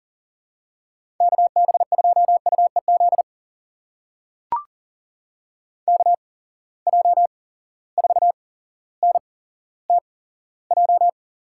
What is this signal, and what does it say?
Morse code, heard as a keyed tone, about 40 wpm, spaced as for 17 wpm, 700 Hz. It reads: KB2UEZ KJ4NTJ
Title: USA Call Signs - 6 Characters - 8x Character Spacing - 40wpm